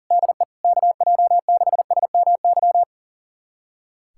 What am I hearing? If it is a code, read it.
Morse code, heard as a keyed tone, about 40 wpm, 700 Hz. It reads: DE KJ6SMY
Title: USA Callsign DE Callsign - 6 Characters - 40wpm